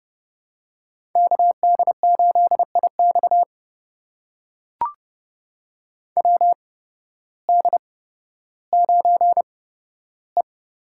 Morse code, heard as a keyed tone, about 30 wpm, spaced as for 14 wpm, 700 Hz. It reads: KD8IX WD9E
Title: USA Call Signs - 8x Character Spacing - 30wpm